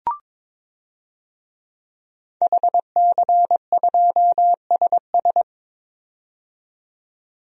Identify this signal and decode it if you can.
Morse code, heard as a keyed tone, about 21 wpm, 700 Hz. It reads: HC2SS
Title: DXCC-100 Call Signs - 5 Characters - 22wpm